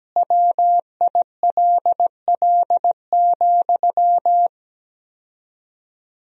Morse code, heard as a keyed tone, about 17 wpm, 700 Hz. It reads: WILL,
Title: 4x Word-Spacing - Sentences from Top 200 Words - 17wpm